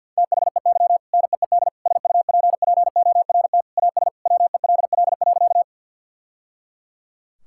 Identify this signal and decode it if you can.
Morse code, heard as a keyed tone, about 52 wpm, 700 Hz. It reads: THEY NEED SUPPORT AS WELL.
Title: Sentences from Top 1000 Words - Review Part 5 - 50wpm